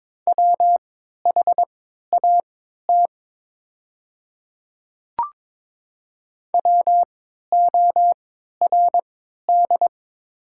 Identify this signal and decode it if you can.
Morse code, heard as a keyed tone, about 22 wpm, spaced as for 13 wpm, 700 Hz. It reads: WHAT WORD
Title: 3x Character Spacing - Top 100 Words - 22wpm